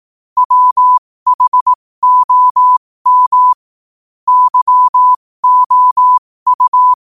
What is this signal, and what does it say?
Morse code, heard as a keyed tone, about 18 wpm, spaced as for 15 wpm, 1000 Hz. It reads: WHOM YOU
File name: qotdfiles / 15wpm